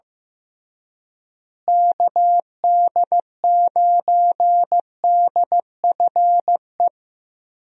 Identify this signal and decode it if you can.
Morse code, heard as a keyed tone, about 15 wpm, 700 Hz. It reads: KD9DFE
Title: USA Call Signs - 6 Characters - Repeated 2x - 15wpm